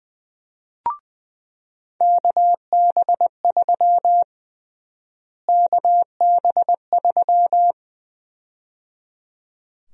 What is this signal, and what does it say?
Morse code, heard as a keyed tone, about 20 wpm, 700 Hz. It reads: KB3 KB3
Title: USA Call Sign Prefixes - Repeated 2x - 20wpm